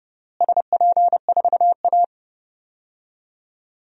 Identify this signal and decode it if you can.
Morse code, heard as a keyed tone, about 32 wpm, 700 Hz. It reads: SP4A
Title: DXCC-100 Call Signs - 4 Characters - Repeated 3x - 30wpm